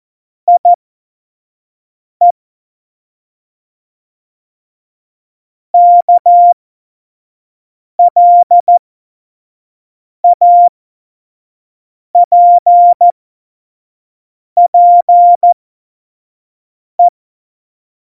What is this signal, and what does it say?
Morse code, heard as a keyed tone, about 14 wpm, spaced as for 5 wpm, 700 Hz. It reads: IE KLAPPE
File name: Morsekurs / Witze / Witz 16. Oktober 2018, 25 F70